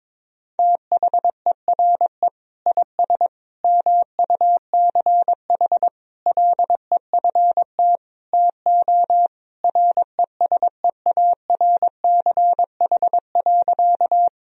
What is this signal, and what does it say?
Morse code, heard as a keyed tone, about 21 wpm, 700 Hz. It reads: THERE IS MUCH LEFT TO RESEARCH.
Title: Sentences from Top 1000 Words - Review Part 4 - 22wpm